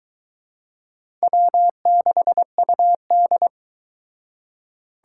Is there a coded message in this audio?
Yes: Morse code from 1.2 to 3.5 s, heard as a keyed tone, about 24 wpm, 700 Hz. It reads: W6UD